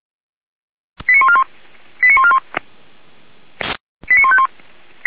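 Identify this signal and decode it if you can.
Five-tone selective call (ZVEI), heard as quick runs of five tones, tones of about 70 ms. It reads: 89141 89141 89151
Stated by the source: ZVEI five-tone selective call received on an SDR.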